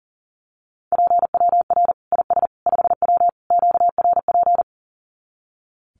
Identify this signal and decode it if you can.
Morse code, heard as a keyed tone, about 40 wpm, 700 Hz. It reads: PWR IS 5W QRP
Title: Common QSO Phrases 40wpm